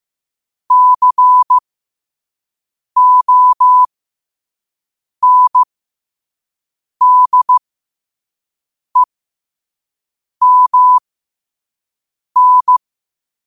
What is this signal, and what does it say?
Morse code, heard as a keyed tone, about 15 wpm, spaced as for 5 wpm, 1000 Hz. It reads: CONDEMN
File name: 5wpm